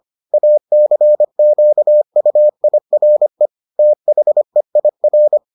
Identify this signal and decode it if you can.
Morse code, heard as a keyed tone, about 25 wpm, 600 Hz. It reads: ACQUIRE THEIR